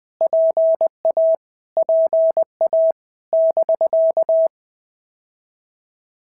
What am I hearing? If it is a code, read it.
Morse code, heard as a keyed tone, about 20 wpm, 650 Hz. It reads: PA PA <BK>
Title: POTA QSO Practice - 20wpm